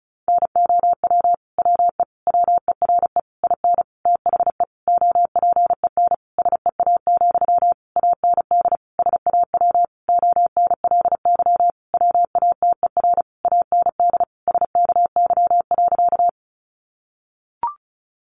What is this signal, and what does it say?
Morse code, heard as a keyed tone, about 34 wpm, 700 Hz. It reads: NOW WE WERE IN THE OPEN SEA, AND SAW ONLY WATER AND SKY.
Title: Sentences from Top 1200 Words 35wpm